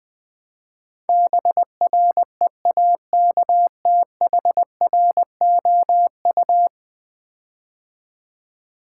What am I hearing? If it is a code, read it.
Morse code, heard as a keyed tone, about 20 wpm, 700 Hz. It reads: BREAKTHROU
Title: Build A Word - 5 Letters At A Time - 12-Letter Words - 20wpm